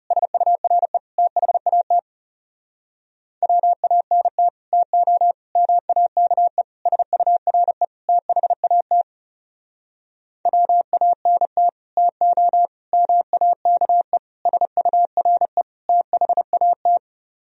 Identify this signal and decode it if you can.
Morse code, heard as a keyed tone, about 36 wpm, 700 Hz. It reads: SURE THAT WANT TO MAKE SURE THAT WANT TO MAKE SURE THAT
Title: Speed-Racing - Sets of 5 Words 30wpm